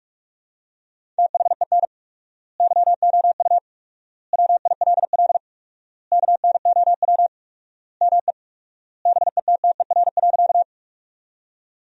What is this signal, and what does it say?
Morse code, heard as a keyed tone, about 46 wpm, 700 Hz. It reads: THEN YOU WILL KNOW ME BETTER.
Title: 4x Word-Spacing - Sentences from Top 400 Words - 45wpm